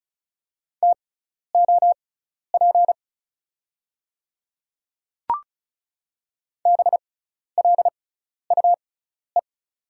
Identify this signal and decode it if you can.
Morse code, heard as a keyed tone, about 36 wpm, spaced as for 12 wpm, 700 Hz. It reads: TOP BLUE
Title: 6x Character Spacing - Top 500 Words - 35wpm